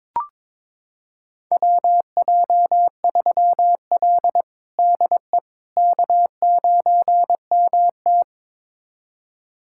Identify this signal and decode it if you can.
Morse code, heard as a keyed tone, about 22 wpm, 700 Hz. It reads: WJ3L DE K9MT
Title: USA Callsign DE Callsign - 4 Characters - 22wpm